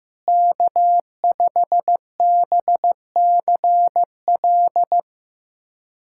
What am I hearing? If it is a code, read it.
Morse code, heard as a keyed tone, about 16 wpm, 700 Hz. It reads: K5BCL
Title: USA Callsign DE Callsign - 5 Characters - 15wpm